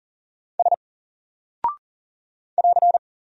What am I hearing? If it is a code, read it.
Morse code, heard as a keyed tone, about 40 wpm, 700 Hz. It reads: S <AR>
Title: Single Character (Letter:Number:Punctuation:Prosign) Alt 40wpm